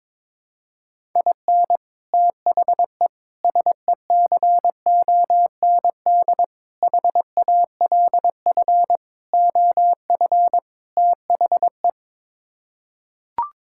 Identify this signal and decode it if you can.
Morse code, heard as a keyed tone, about 22 wpm, 700 Hz. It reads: IN THE SECOND HALF OF THE
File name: practice / 22wpm / Sets of 6 Words - 22wpm